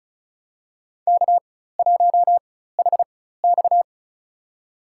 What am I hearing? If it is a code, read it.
Morse code, heard as a keyed tone, about 34 wpm, spaced as for 16 wpm, 700 Hz. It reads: K1HX
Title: USA Call Signs - 4 Characters - 4x Character Spacing - 35wpm